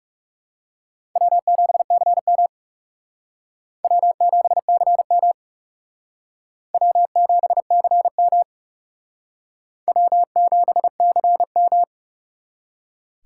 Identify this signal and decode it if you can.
Morse code, heard as a keyed tone, about 38 wpm, 700 Hz. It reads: W7CM W7CM W7CM W7CM